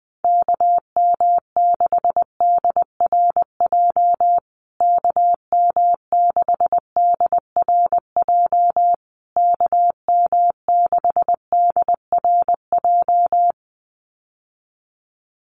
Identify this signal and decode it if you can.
Morse code, heard as a keyed tone, about 20 wpm, 700 Hz. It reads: KM6DRJ KM6DRJ KM6DRJ
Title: USA Call Signs Repeated 3x 20wpm